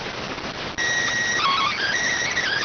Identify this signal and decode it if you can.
SSTV (slow-scan television), PD 290.